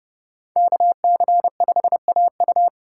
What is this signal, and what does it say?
Morse code, heard as a keyed tone, about 30 wpm, 700 Hz. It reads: KC5AU
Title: USA Call Signs - 5 Characters - 4x Character Spacing - 30wpm